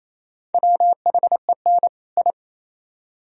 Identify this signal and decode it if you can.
Morse code, heard as a keyed tone, about 37 wpm, 700 Hz. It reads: WHEN I